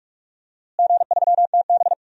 Morse code, heard as a keyed tone, about 44 wpm, 700 Hz. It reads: G3TB
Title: DXCC-100 Call Signs - 4 Characters - Repeated 2x - 45wpm